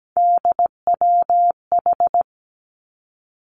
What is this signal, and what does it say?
Morse code, heard as a keyed tone, about 17 wpm, 700 Hz. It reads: DWH
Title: Three Challenging Letter-Number Combinations 17wpm